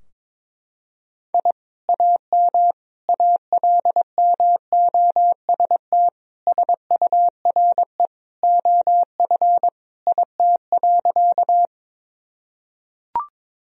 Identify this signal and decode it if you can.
Morse code, heard as a keyed tone, about 21 wpm, 700 Hz. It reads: I AM ALMOST SURE OF IT.
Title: Sentences from Top 1000 Words - Review Part 4 - 22wpm